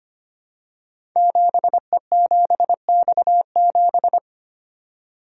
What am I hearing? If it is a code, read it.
Morse code, heard as a keyed tone, about 25 wpm, 700 Hz. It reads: 7E7X7
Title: Code Groups - 5 Letter:Numbers - 25wpm